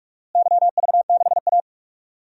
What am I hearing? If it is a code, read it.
Morse code, heard as a keyed tone, about 46 wpm, 700 Hz. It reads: YV6A
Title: DXCC-100 Call Signs - 4 Characters - Repeated 2x - 45wpm